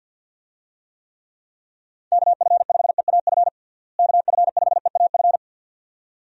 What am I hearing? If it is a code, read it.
Morse code, heard as a keyed tone, about 51 wpm, 700 Hz. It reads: KF5EAF KF5EAF
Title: USA Call Signs - 6 Characters - Repeated 2x - 50wpm